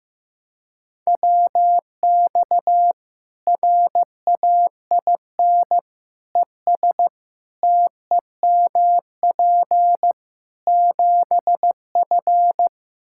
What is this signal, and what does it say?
Morse code, heard as a keyed tone, about 15 wpm, 700 Hz. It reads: WX RAIN ES TEMP 7F
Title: Practice QSO - Exchange 2 - Info - 15wpm